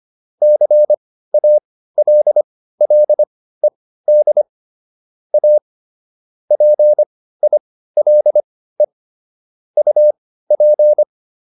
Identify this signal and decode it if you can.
Morse code, heard as a keyed tone, about 25 wpm, spaced as for 15 wpm, 600 Hz. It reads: CALLED A PILE UP